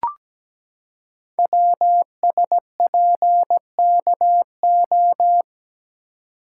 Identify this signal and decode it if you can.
Morse code, heard as a keyed tone, about 17 wpm, 700 Hz. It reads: WSPKO
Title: Code Groups - 5 Letters - 17wpm